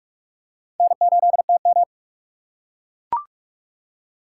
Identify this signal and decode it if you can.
Morse code, heard as a keyed tone, about 45 wpm, 700 Hz. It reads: N8TM
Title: USA Call Signs - 3x Character Spacing - 45wpm